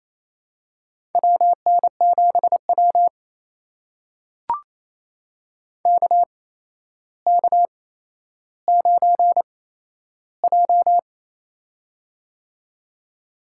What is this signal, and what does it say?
Morse code, heard as a keyed tone, about 29 wpm, spaced as for 12 wpm, 700 Hz. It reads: WN7W KK9J
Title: USA Call Signs - 4 Characters - 8x Character Spacing - 28wpm